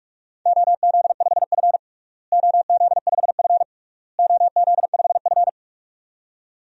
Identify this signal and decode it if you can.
Morse code, heard as a keyed tone, about 46 wpm, 700 Hz. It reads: OZ5F OZ5F OZ5F
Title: DXCC-100 Call Signs - 4 Characters - Repeated 3x - 45wpm